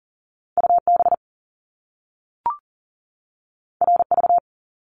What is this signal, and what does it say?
Morse code, heard as a keyed tone, about 40 wpm, 700 Hz. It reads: UB RV
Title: Top Two Letter Combinations in Words 40wpm